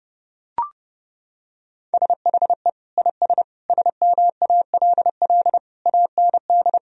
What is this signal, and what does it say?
Morse code, heard as a keyed tone, about 30 wpm, 700 Hz. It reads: SHE IS SMALL AND
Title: Sentences from Top 1000 Words - Review Part 1 - 30wpm